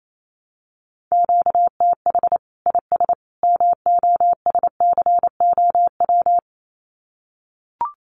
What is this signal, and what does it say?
Morse code, heard as a keyed tone, about 28 wpm, 700 Hz. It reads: QTH IS MOSCOW
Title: Common QSO Phrases 28wpm